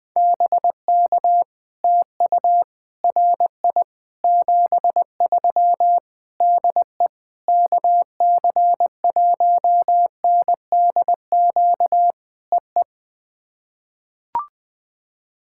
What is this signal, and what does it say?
Morse code, heard as a keyed tone, about 20 wpm, 700 Hz. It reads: BK TU RI 73 DE KC1NDQ EE
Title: POTA QSO Practice - 20wpm